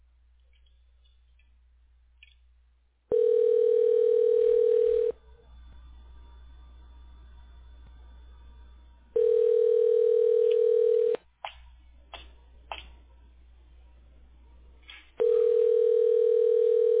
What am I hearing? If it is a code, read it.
Ringback tone (North America), 440 + 480 Hz.